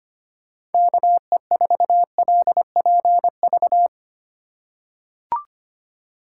Morse code, heard as a keyed tone, about 25 wpm, 700 Hz. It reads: KE4LPV
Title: USA Call Signs - 6 Characters - 4x Character Spacing - 25wpm